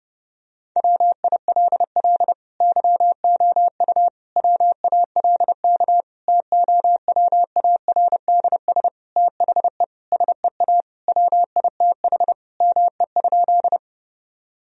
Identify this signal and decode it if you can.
Morse code, heard as a keyed tone, about 30 wpm, 700 Hz. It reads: WILL YOU WALK TOWARDS THE SEA WITH ME?
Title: Sentences from Top 1000 Words - Review Part 2 - 30wpm